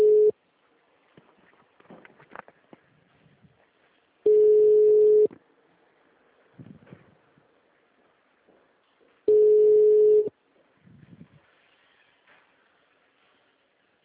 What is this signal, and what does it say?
Ringback tone (Europe, 425 Hz).